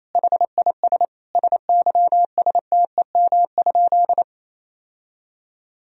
Morse code, heard as a keyed tone, about 27 wpm, 700 Hz. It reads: HIS SYSTEM?
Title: Sentences from Top 1000 Words - Review Part 3 - 28wpm